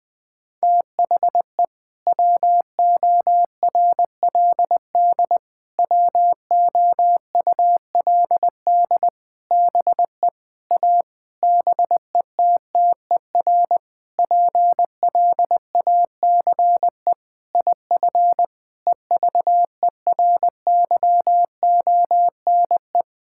Morse code, heard as a keyed tone, about 20 wpm, 700 Hz. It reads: THE WORLD WOULD BE A BETTER PLACE IF EVERYONE
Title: Sentences with 8-Word Phrases - 20wpm